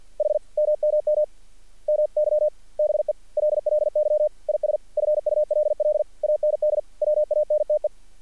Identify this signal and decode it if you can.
Morse code, heard as a keyed tone, about 48 wpm, 600 Hz. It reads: H MMM MY BELLY IS FULL AND WANTE
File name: archived / CRSnet